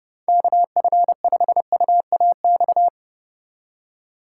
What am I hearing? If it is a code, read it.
Morse code, heard as a keyed tone, about 33 wpm, 700 Hz. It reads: KF5UAX